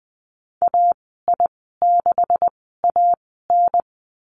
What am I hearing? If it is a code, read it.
Morse code, heard as a keyed tone, about 20 wpm, spaced as for 15 wpm, 700 Hz. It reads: AI6AN
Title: 161 Lesson 26g - Letter g - Call Signs 20wpm